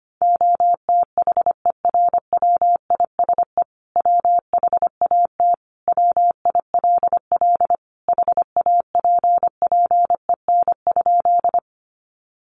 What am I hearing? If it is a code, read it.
Morse code, heard as a keyed tone, about 25 wpm, 700 Hz. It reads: OTHERWISE WHAT WILL HAPPEN?
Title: Sentences from Top 1200 Words 25wpm